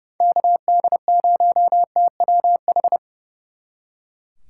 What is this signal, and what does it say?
Morse code, heard as a keyed tone, about 37 wpm, 700 Hz. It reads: KD0TWH
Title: Speed Racing - USA 2x3 Call Signs 30wpm